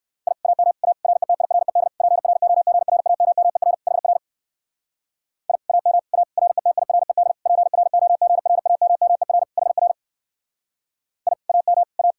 Morse code, heard as a keyed tone, about 59 wpm, 700 Hz. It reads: I AM A RETIRED PROGRAMMER HR I AM A RETIRED PROGRAMMER HR I AM A